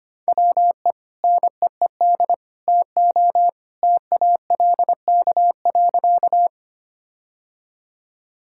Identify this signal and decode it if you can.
Morse code, heard as a keyed tone, about 25 wpm, 700 Hz. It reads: WE NEED TO TALK.
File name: practice / 25wpm / Sentences with 2-Word Phrases - 25wpm